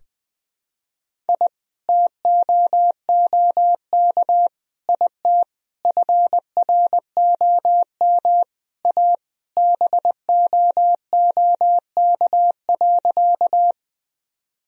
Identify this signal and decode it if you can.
Morse code, heard as a keyed tone, about 21 wpm, 700 Hz. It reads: I TOOK IT FROM A BOOK.
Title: Sentences from Top 1000 Words - Review Part 1 - 20wpm